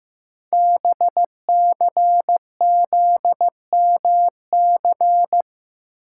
Morse code, heard as a keyed tone, about 15 wpm, 700 Hz. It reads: BCZMC
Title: Code Groups - 5 Letters - 15wpm